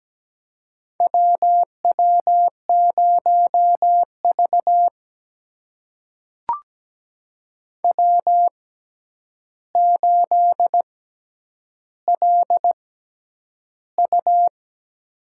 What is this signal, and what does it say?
Morse code, heard as a keyed tone, about 17 wpm, spaced as for 9 wpm, 700 Hz. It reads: WW0V W8LU
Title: USA Call Signs - 4 Characters - 6x Character Spacing - 17wpm